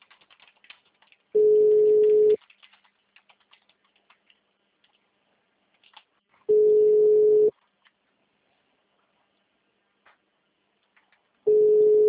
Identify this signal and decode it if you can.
Ringback tone (Europe, 425 Hz).
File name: out- -2013- - - .57107